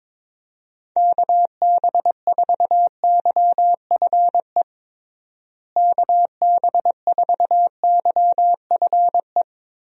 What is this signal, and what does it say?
Morse code, heard as a keyed tone, about 22 wpm, 700 Hz. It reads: KB4YFE KB4YFE
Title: USA Call Signs - 6 Characters - Repeated 2x - 22wpm